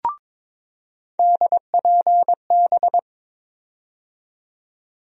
Morse code, heard as a keyed tone, about 22 wpm, 700 Hz. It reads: DPB
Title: USA Call Sign Postfixes - 3 Letters 22wpm